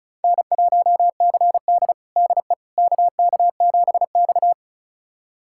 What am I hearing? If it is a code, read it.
Morse code, heard as a keyed tone, about 36 wpm, 700 Hz. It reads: N1CD DE KK7X
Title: USA Callsign DE Callsign - 4 Characters - 35wpm